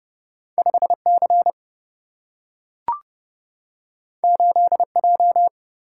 Morse code, heard as a keyed tone, about 29 wpm, 700 Hz. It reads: 5C 8J
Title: Single-Letter Single-Number 30wpm